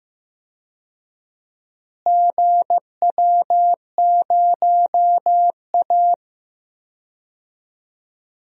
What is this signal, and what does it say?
Morse code, heard as a keyed tone, about 15 wpm, 700 Hz. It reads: GW0A
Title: DXCC-100 Call Signs - 4 Characters - Repeated 2x - 15wpm